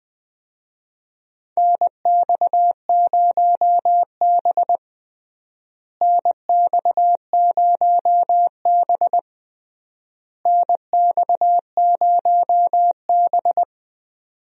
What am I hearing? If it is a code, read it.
Morse code, heard as a keyed tone, about 20 wpm, 700 Hz. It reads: NX0B NX0B NX0B
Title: USA Call Signs - 4 Characters - Repeated 3x - 20wpm